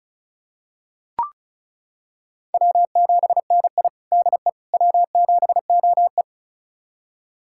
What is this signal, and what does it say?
Morse code, heard as a keyed tone, about 36 wpm, 700 Hz. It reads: W7NI DE W7OE
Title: USA Callsign DE Callsign - 4 Characters - 35wpm